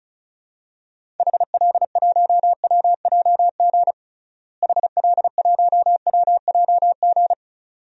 Morse code, heard as a keyed tone, about 34 wpm, 700 Hz. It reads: HL1WJG HL1WJG
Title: DXCC-100 Call Signs - 6 Characters - Repeated 2x - 35wpm